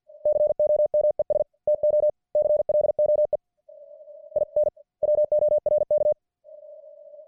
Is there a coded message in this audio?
Yes: Morse code from 0.3 to 6.1 s, heard as a keyed tone, about 50 wpm, 600 Hz. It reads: COMES TO CLOE IN WORK